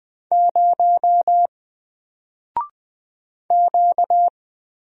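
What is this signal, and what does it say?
Morse code, heard as a keyed tone, about 20 wpm, 700 Hz. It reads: 0 Q
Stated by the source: Single letter and number Morse drill at 20 wpm, from N4LCD.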